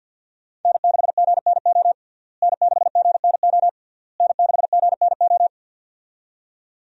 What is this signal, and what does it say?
Morse code, heard as a keyed tone, about 53 wpm, 700 Hz. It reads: N6GNO N6GNO N6GNO
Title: USA Call Signs - 5 Characters - Repeated 3x - 50wpm